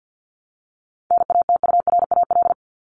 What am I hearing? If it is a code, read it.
Morse code, heard as a keyed tone, about 48 wpm, 700 Hz. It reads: NATURAL
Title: LY Suffix - Top 100 Words with Base Words 50wpm